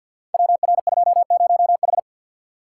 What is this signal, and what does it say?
Morse code, heard as a keyed tone, about 52 wpm, 700 Hz. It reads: WR20H
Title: Code Groups - 5 Letter:Numbers - 50wpm